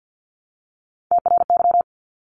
Morse code, heard as a keyed tone, about 50 wpm, 700 Hz. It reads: TRY
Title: Top 200 Words 50wpm